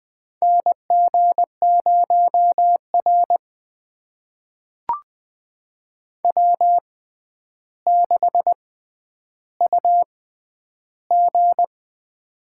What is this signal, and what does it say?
Morse code, heard as a keyed tone, about 20 wpm, spaced as for 10 wpm, 700 Hz. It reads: NG0R W6UG
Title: USA Call Signs - 6x Character Spacing - 20wpm